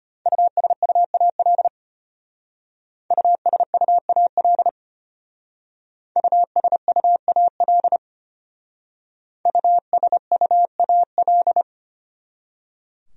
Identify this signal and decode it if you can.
Morse code, heard as a keyed tone, about 32 wpm, 700 Hz. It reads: USUAL USUAL USUAL USUAL